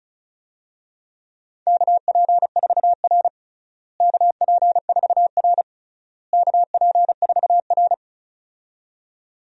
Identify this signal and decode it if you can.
Morse code, heard as a keyed tone, about 34 wpm, 700 Hz. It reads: KP4R KP4R KP4R
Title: DXCC-100 Call Signs - 4 Characters - Repeated 3x - 35wpm